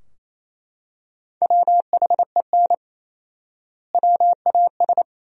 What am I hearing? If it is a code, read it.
Morse code, heard as a keyed tone, about 27 wpm, 700 Hz. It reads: WHEN WAS
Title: 4x Word-Spacing - Sentences from Top 100 Words - 28wpm